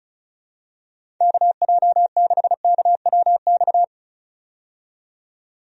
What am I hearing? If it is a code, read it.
Morse code, heard as a keyed tone, about 34 wpm, 700 Hz. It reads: KJ6KWX
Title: USA Call Signs - 6 Characters - 35wpm